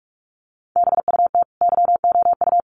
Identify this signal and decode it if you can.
Morse code, heard as a keyed tone, about 44 wpm, 700 Hz. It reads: BUT YOU
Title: Sets of 5 Words Encore 45wpm